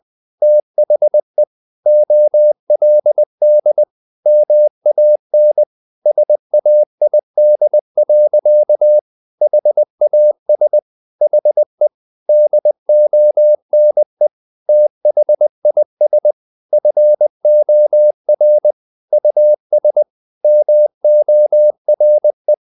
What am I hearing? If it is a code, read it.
Morse code, heard as a keyed tone, about 20 wpm, 600 Hz. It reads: THE OLD MAN SAID. HAS HE DONE THIS FOR US MORE
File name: the old man and the sea / the old man and the sea part 1 20wpm